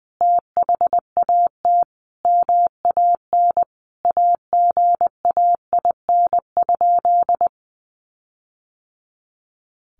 Morse code, heard as a keyed tone, about 20 wpm, 700 Hz. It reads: THAT MAN AGAIN?
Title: Sentences from Top 1400 Words 20wpm